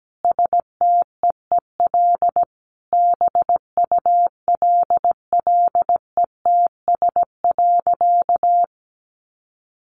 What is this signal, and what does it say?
Morse code, heard as a keyed tone, about 17 wpm, 700 Hz. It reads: STEEL BULLETS.
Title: A Sound Of Thunder 17wpm